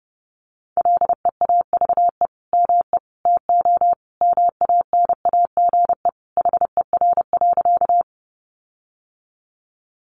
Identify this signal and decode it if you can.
Morse code, heard as a keyed tone, about 30 wpm, 700 Hz. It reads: LEAVE ME TO MANAGE HER.
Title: Sentences from Top 1400 Words 30wpm